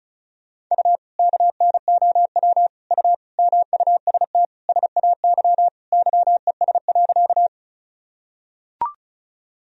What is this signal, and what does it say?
Morse code, heard as a keyed tone, about 36 wpm, 700 Hz. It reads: U KNOW U MUST SAY YES.
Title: Sentences from Top 500 Words and Abbreviations 35wpm